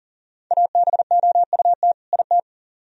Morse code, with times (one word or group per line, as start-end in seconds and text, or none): 0.5-1.9 ABOUT
2.1-2.4 IT